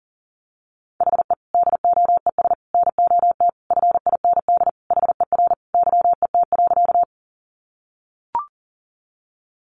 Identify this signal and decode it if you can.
Morse code, heard as a keyed tone, about 40 wpm, 700 Hz. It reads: HE DOES NOT FIND HER YET.
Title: Easy Phrases 40wpm